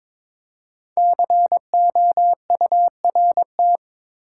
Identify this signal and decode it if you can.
Morse code, heard as a keyed tone, about 22 wpm, 700 Hz. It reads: COURT
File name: 5 Letter Words - First 3 Letters Given - 22wpm